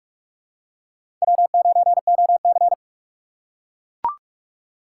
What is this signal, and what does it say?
Morse code, heard as a keyed tone, about 43 wpm, 700 Hz. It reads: W9OC
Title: USA Call Signs - 4 Characters - 6x Character Spacing - 45wpm